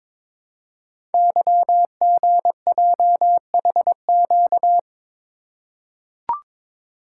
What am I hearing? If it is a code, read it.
Morse code, heard as a keyed tone, about 23 wpm, 700 Hz. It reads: YGJHQ
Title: Code Groups - 5 Letters - 22wpm